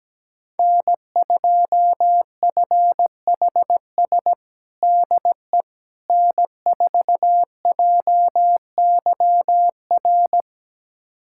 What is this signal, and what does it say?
Morse code, heard as a keyed tone, about 17 wpm, 700 Hz. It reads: N2FHS DE N4JYR
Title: USA Callsign DE Callsign - 5 Characters - 17wpm